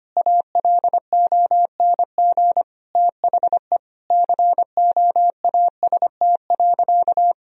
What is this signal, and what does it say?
Morse code, heard as a keyed tone, about 25 wpm, 700 Hz. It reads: ALONG THE COAST.
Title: Sentences from Top 1000 Words - Review Part 4 - 25wpm